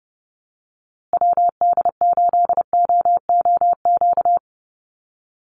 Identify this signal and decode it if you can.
Morse code, heard as a keyed tone, about 30 wpm, 700 Hz. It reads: WD8OOQ
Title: USA Call Signs Repeated 2x 30wpm